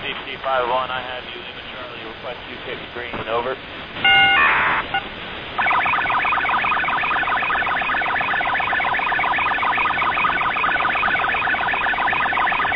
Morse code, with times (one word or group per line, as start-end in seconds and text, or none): none